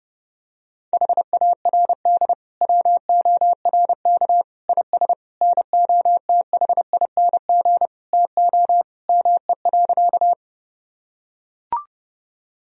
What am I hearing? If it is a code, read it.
Morse code, heard as a keyed tone, about 30 wpm, 700 Hz. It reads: HARD WORK IS NOTHING TO ME.
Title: Sentences from Top 1000 Words - Review Part 1 - 30wpm